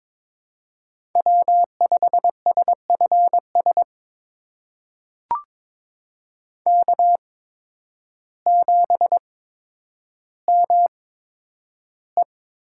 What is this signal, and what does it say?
Morse code, heard as a keyed tone, about 22 wpm, spaced as for 10 wpm, 700 Hz. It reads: W5SFS K7ME
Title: USA Call Signs - 8x Character Spacing - 22wpm